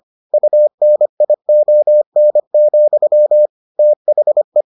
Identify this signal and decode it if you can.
Morse code, heard as a keyed tone, about 25 wpm, 600 Hz. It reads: UNION, THE